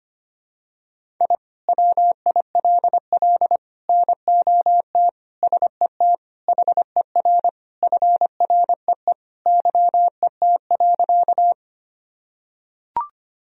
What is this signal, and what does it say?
Morse code, heard as a keyed tone, about 25 wpm, 700 Hz. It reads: I WILL NOT SET HER FREE YET.
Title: Sentences from Top 1000 Words - Review Part 1 - 25wpm